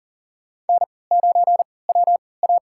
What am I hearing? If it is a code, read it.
Morse code, heard as a keyed tone, about 40 wpm, spaced as for 23 wpm, 700 Hz. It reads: N9WA
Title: USA Call Signs - 4 Characters - 3x Character Spacing - 40wpm